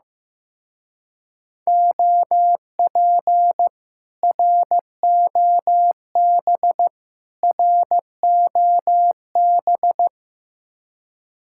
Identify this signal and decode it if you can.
Morse code, heard as a keyed tone, about 16 wpm, 700 Hz. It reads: OP ROB ROB
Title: Speed-Racing Common QSO Phrases Encore - 15wpm